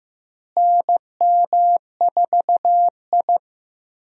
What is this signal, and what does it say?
Morse code, heard as a keyed tone, about 15 wpm, 700 Hz. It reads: NM4I
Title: USA Call Signs - 4 Characters - Repeated 3x - 15wpm